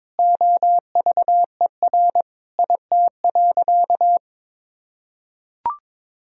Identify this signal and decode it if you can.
Morse code, heard as a keyed tone, about 22 wpm, 700 Hz. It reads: OVER IT.
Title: Sentences from Top 1000 Words - Review Part 3 - 22wpm